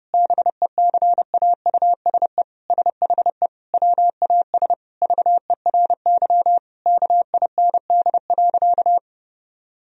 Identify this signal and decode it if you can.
Morse code, heard as a keyed tone, about 32 wpm, 700 Hz. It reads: BECAUSE SHE WAS VERY KIND.
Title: Sentences from Top 1000 Words - Review Part 3 - 30wpm